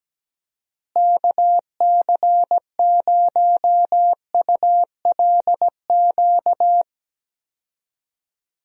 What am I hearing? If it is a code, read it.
Morse code, heard as a keyed tone, about 17 wpm, 700 Hz. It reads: KC0ULQ
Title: USA Call Signs - 6 Characters - Repeated 3x - 17wpm